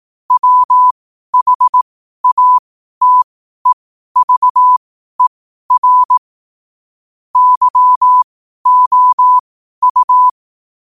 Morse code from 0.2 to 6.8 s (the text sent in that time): WHATEVER